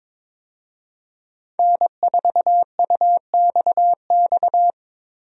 Morse code, heard as a keyed tone, about 22 wpm, 700 Hz. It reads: N4UXX
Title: USA Call Signs - 5 Characters - 22wpm